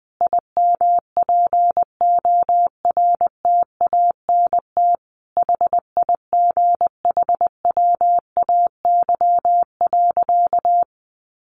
Morse code, heard as a keyed tone, about 20 wpm, 700 Hz. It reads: IMPORTANT HIGHWAY.
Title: Sentences from Top 1400 Words 20wpm